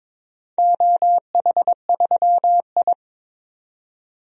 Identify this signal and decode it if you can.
Morse code, heard as a keyed tone, about 22 wpm, 700 Hz. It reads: OH3I
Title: DXCC-100 Call Signs - 4 Characters - 22wpm